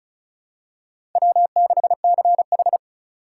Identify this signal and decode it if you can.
Morse code, heard as a keyed tone, about 34 wpm, 700 Hz. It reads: W6CH
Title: USA Call Signs - 4 Characters - 2x Character Spacing - 35wpm